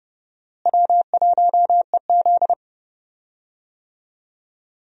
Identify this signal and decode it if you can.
Morse code, heard as a keyed tone, about 32 wpm, 700 Hz. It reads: W1EZ
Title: USA Call Signs - 4 Characters - Repeated 2x - 30wpm